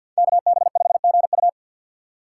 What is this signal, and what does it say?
Morse code, heard as a keyed tone, about 50 wpm, 700 Hz. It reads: KB5GU
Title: USA Call Signs - 5 Characters - 4x Character Spacing - 50wpm